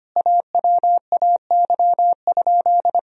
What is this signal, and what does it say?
Morse code, heard as a keyed tone, about 25 wpm, 700 Hz. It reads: AWAY?
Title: Sentences from Top 1000 Words - Review Part 1 - 25wpm